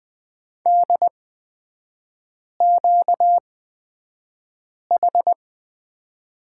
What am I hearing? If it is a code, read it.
Morse code, heard as a keyed tone, about 20 wpm, spaced as for 5 wpm, 700 Hz. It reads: DQH
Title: 236 Lesson 38e - Letter q - Three Characters 20wpm